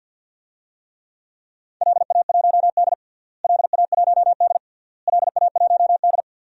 Morse code, heard as a keyed tone, about 47 wpm, 700 Hz. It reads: LA1D LA1D LA1D
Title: DXCC-100 Call Signs - 4 Characters - Repeated 3x - 50wpm